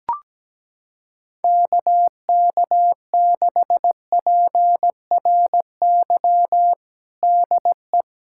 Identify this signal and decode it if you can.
Morse code, heard as a keyed tone, about 17 wpm, 700 Hz. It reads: KK6PRY DE
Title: USA Callsign DE Callsign - 6 Characters - 17wpm